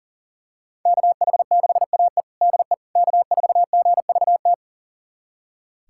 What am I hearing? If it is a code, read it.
Morse code, heard as a keyed tone, about 37 wpm, 700 Hz. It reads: KH6AE DE K4GVT
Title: USA Callsign DE Callsign - 5 Characters - 40wpm